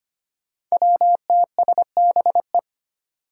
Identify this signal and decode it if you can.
Morse code, heard as a keyed tone, about 25 wpm, 700 Hz. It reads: WTSBE
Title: Five Challenging Letter-Number Combinations 25wpm